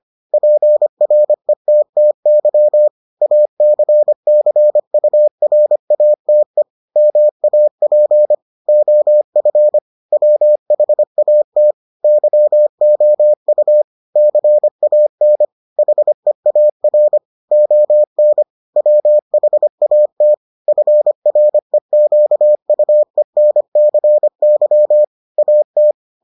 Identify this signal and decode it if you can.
Morse code, heard as a keyed tone, about 25 wpm, 600 Hz. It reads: PRETTY ACCURATE MAP OF WHAT YOU CAN HEAR ON WHAT FREQUENCY AT